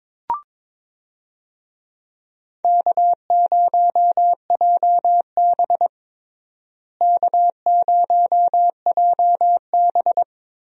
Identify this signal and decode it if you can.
Morse code, heard as a keyed tone, about 22 wpm, 700 Hz. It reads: K0JB K0JB
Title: USA Call Signs - 4 Characters - Repeated 3x - 22wpm